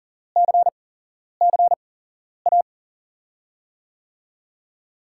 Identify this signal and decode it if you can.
Morse code, heard as a keyed tone, about 38 wpm, spaced as for 11 wpm, 700 Hz. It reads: CCA
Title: USA Call Signs - 8x Character Spacing - 40wpm